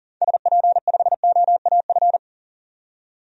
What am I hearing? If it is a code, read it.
Morse code, heard as a keyed tone, about 40 wpm, 700 Hz. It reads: SP5OAF
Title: DXCC-100 Call Signs - 6 Characters - 40wpm